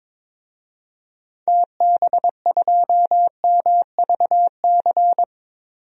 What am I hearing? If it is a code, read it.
Morse code, heard as a keyed tone, about 21 wpm, 700 Hz. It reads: TB2MVC
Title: DXCC-100 Call Signs - 6 Characters - 22wpm